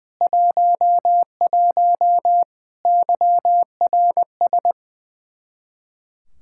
Morse code, heard as a keyed tone, about 21 wpm, 700 Hz. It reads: 11 YRS
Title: Practice QSO - Exchange 3 - More Info - 20wpm